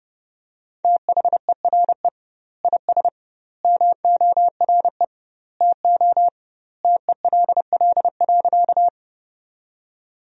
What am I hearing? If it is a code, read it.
Morse code, heard as a keyed tone, about 30 wpm, 700 Hz. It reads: THERE IS MORE TO TELL.
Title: 2x Word-Spacing - Sentences from Top 200 Words - 30wpm